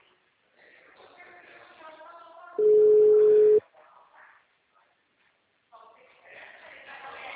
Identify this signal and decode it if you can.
Ringback tone (Europe, 425 Hz), with speech.